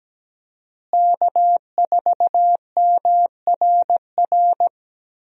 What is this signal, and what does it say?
Morse code, heard as a keyed tone, about 17 wpm, 700 Hz. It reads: K4MRR
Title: USA Call Signs - 5 Characters - Repeated 3x - 17wpm